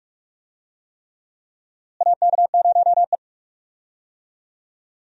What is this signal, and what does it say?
Morse code, heard as a keyed tone, about 46 wpm, 700 Hz. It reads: AK0E
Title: USA Call Signs - 4 Characters - Repeated 3x - 45wpm